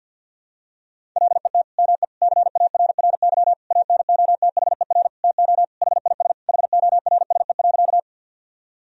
Morse code, heard as a keyed tone, about 46 wpm, 700 Hz. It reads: LET ME CARRY ANOTHER TO HIS HORSE.
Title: Sentences from Top 1000 Words - Review Part 1 - 50wpm